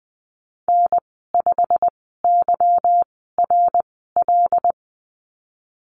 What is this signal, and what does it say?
Morse code, heard as a keyed tone, about 20 wpm, spaced as for 15 wpm, 700 Hz. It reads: N5YRL
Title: 167 Lesson 27g - Number 7 - Call Signs 20wpm